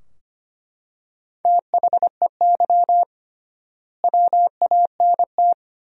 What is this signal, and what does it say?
Morse code, heard as a keyed tone, about 25 wpm, 700 Hz. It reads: THEY WANT
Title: 3x Word-Spacing - Sentences from Top 200 Words - 25wpm